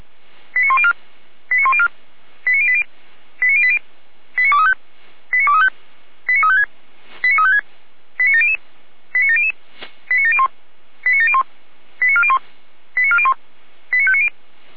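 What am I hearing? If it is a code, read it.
Five-tone selective call (ZVEI), heard as quick runs of five tones, tones of about 70 ms. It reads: 89104 89104 89080 89080 89235 89235 89356 89356 89700 89700 89701 89701 89401 89401 89590
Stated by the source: ZVEI five-tone selective call received on an SDR.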